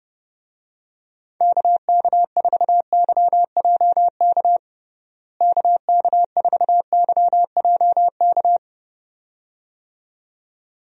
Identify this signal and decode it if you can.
Morse code, heard as a keyed tone, about 30 wpm, 700 Hz. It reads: KK4YJK KK4YJK
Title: USA Call Signs - 6 Characters - Repeated 2x - 30wpm